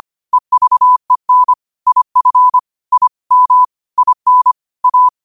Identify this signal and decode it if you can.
Morse code, heard as a keyed tone, about 25 wpm, 1000 Hz. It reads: EVEN IF I M IN A